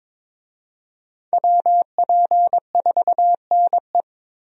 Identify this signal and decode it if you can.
Morse code, heard as a keyed tone, about 22 wpm, 700 Hz. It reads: WP4NE
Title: USA Call Signs - 5 Characters - Repeated 2x - 22wpm